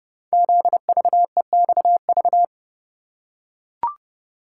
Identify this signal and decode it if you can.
Morse code, heard as a keyed tone, about 29 wpm, 700 Hz. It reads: ZVEXV
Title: Code Groups - 5 Letters - 30wpm